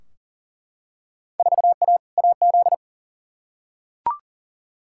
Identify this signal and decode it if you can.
Morse code, heard as a keyed tone, about 37 wpm, 700 Hz. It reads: 4A AZ
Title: Field Day Exchanges Encore - 40wpm